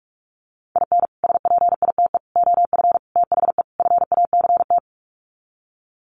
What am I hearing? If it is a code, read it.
Morse code, heard as a keyed tone, about 44 wpm, 700 Hz. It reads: IN SPITE OF THE FACT
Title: Sets of 5 Words Encore2 45wpm